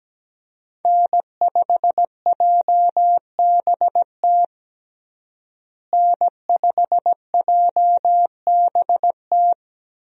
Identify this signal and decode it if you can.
Morse code, heard as a keyed tone, about 17 wpm, 700 Hz. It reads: N5JBT N5JBT
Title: USA Call Signs - 5 Characters - Repeated 3x - 17wpm